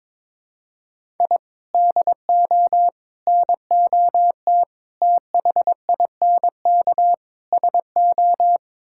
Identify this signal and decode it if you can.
Morse code, heard as a keyed tone, about 22 wpm, 700 Hz. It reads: I DO NOT THINK SO
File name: practice / 22wpm / Build A Phrase - 5-Word Phrases - 22wpm